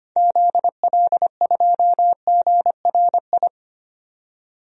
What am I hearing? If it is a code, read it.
Morse code, heard as a keyed tone, about 25 wpm, 700 Hz. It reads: ZL2GRI
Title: DXCC-100 Call Signs - 6 Characters - 25wpm